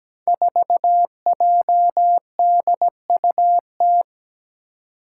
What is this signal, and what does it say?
Morse code, heard as a keyed tone, about 17 wpm, 700 Hz. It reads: 4JDUT
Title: Five Challenging Letter-Number Combinations 17wpm